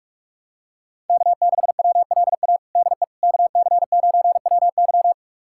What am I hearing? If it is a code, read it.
Morse code, heard as a keyed tone, about 46 wpm, 700 Hz. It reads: K6WLA DE KC9WY
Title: USA Callsign DE Callsign 45wpm